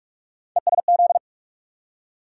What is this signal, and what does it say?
Morse code, heard as a keyed tone, about 43 wpm, 700 Hz. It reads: ESZ
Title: USA Call Sign Postfixes - 3 Letters 45wpm